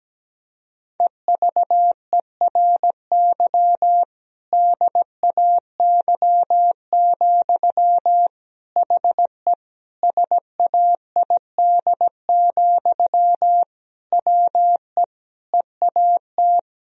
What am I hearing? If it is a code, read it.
Morse code, heard as a keyed tone, about 17 wpm, 700 Hz. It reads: EVERY DAY, HE SAID, WE EAT
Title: Sentences from Top 1000 Words - Review Part 1 - 17wpm